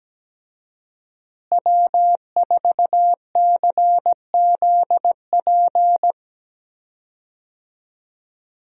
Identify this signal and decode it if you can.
Morse code, heard as a keyed tone, about 17 wpm, 700 Hz. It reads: W4CZP
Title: USA Call Signs - 5 Characters - Repeated 3x - 17wpm